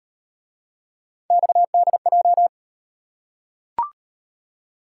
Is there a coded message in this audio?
Yes: Morse code from 1.3 to 2.5 s, heard as a keyed tone, about 37 wpm, 700 Hz. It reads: XDJ